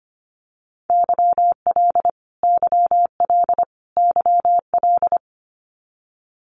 Morse code, heard as a keyed tone, about 25 wpm, 700 Hz. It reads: YL YL YL
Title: Abbreviations Repeated 3x - 25wpm